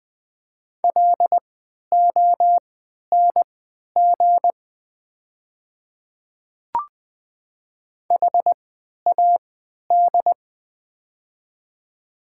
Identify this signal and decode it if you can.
Morse code, heard as a keyed tone, about 20 wpm, spaced as for 11 wpm, 700 Hz. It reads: LONG HAD
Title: 3x Character Spacing - Top 100 Words - 20wpm